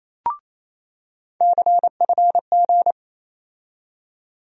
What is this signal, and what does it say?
Morse code, heard as a keyed tone, about 28 wpm, 700 Hz. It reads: CFG
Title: USA Call Sign Postfixes - 3 Letters 28wpm